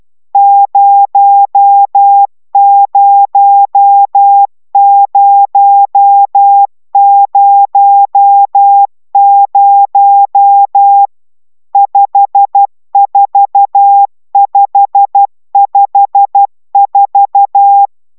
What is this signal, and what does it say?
Morse code, heard as a keyed tone, about 12 wpm, 800 Hz. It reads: 00000 54554
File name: taviro tanulas / cw 69 20 10